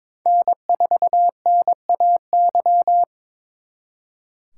Morse code, heard as a keyed tone, about 26 wpm, 700 Hz. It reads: N4NAY
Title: Speed Racing - USA 1x3 Call Signs 22wpm